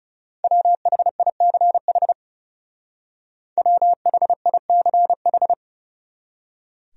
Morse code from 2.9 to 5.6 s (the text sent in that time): WHICH